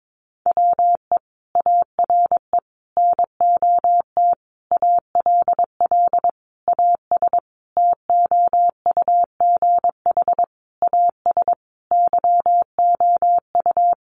Morse code, heard as a keyed tone, about 22 wpm, 700 Hz. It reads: WE ARE NOT ALL AS TOUGH AS YOU
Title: Sentences from Top 1200 Words 22wpm